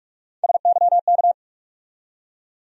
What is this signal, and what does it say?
Morse code, heard as a keyed tone, about 41 wpm, 700 Hz. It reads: SYK
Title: USA Call Sign Postfixes - 3 Letters 45wpm